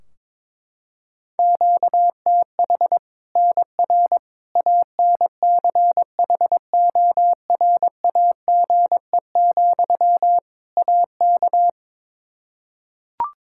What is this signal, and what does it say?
Morse code, heard as a keyed tone, about 22 wpm, 700 Hz. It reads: QTH NR ANCHORAGE, AK
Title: K1USN SST QSO Practice 22wpm